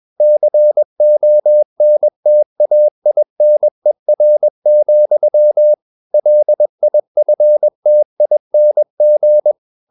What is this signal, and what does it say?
Morse code, heard as a keyed tone, about 21 wpm, 600 Hz. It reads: CONTAINER, LIFTING
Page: CW med Gnister